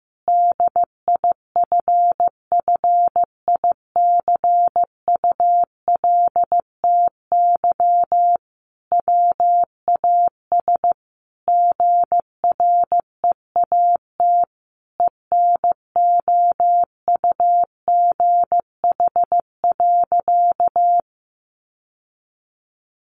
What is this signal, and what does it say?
Morse code, heard as a keyed tone, about 15 wpm, 700 Hz. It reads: DIFFICULTY WAS GREAT ENOUGH.
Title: Sentences from Top 1500 Words 15wpm